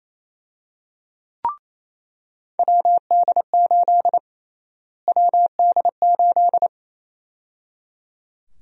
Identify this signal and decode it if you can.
Morse code, heard as a keyed tone, about 28 wpm, 700 Hz. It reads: WD8 WD8
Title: USA Call Sign Prefixes - Repeated 2x - 28wpm